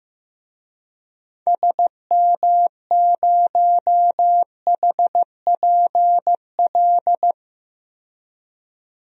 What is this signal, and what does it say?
Morse code, heard as a keyed tone, about 15 wpm, 700 Hz. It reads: SM0HPL
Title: DXCC-100 Call Signs - 6 Characters - 15wpm